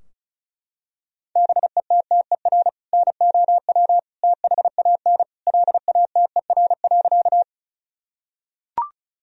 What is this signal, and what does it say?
Morse code, heard as a keyed tone, about 36 wpm, 700 Hz. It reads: BETTER NOW THAN LATER.
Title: Sentences from Top 1000 Words - Review Part 4 - 35wpm